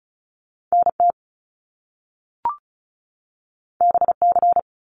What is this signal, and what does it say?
Morse code, heard as a keyed tone, about 36 wpm, 700 Hz. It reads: NT BC
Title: Canadian Provinces and Territories 35wpm